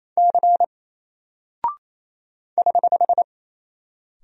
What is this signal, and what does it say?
Morse code, heard as a keyed tone, about 28 wpm, 700 Hz. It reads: C <HH>
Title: Single Character (Letter:Number:Punctuation:Prosign) 28wpm